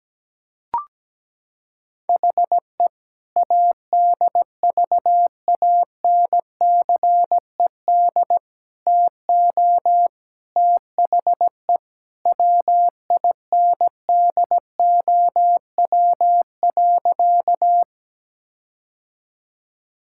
Morse code, heard as a keyed tone, about 17 wpm, 700 Hz. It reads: HE ADVANCED TO THE WINDOW.
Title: Sentences from Top 1600 Words 17wpm